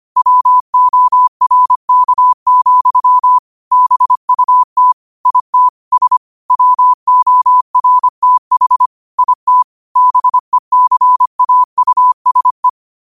Morse code, heard as a keyed tone, about 25 wpm, 1000 Hz. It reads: WORK, BUT IT S WORTH IT BECAUSE